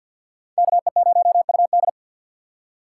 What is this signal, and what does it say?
Morse code, heard as a keyed tone, about 64 wpm, 700 Hz. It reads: KE0UD